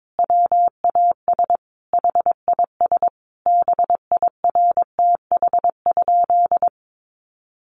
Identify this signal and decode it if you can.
Morse code, heard as a keyed tone, about 22 wpm, 700 Hz. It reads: WAS HIS BIRTH?
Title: Sentences from Top 800 Words 22wpm